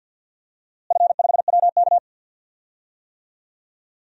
Morse code, heard as a keyed tone, about 52 wpm, 700 Hz. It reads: F5WK
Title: DXCC-100 Call Signs - 4 Characters - Repeated 2x - 50wpm